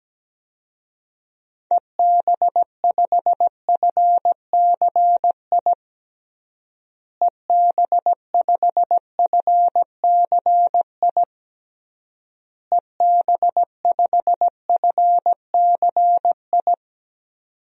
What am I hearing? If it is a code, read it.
Morse code, heard as a keyed tone, about 17 wpm, 700 Hz. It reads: EB5FCI EB5FCI EB5FCI
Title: DXCC-100 Call Signs - 6 Characters - Repeated 3x - 17wpm